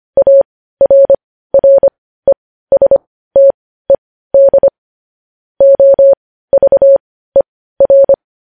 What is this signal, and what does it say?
Morse code, heard as a keyed tone, about 25 wpm, spaced as for 15 wpm, 550 Hz. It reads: ARRESTED OVER
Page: News Headlines in Morse Code at 15 WPM